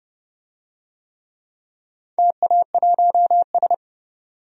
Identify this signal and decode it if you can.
Morse code, heard as a keyed tone, about 30 wpm, 700 Hz. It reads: TA1S
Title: DXCC-100 Call Signs - 4 Characters - 30wpm